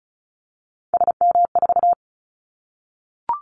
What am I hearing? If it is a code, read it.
Morse code, heard as a keyed tone, about 34 wpm, 700 Hz. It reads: SM4
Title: Callsign Prefixes for Top 25 DXCC - Repeated 3x - 35wpm